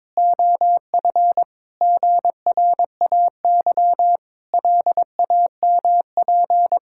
Morse code, heard as a keyed tone, about 22 wpm, 700 Hz. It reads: OF GRAY LAMP
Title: Pangram Challenge v2 - 22wpm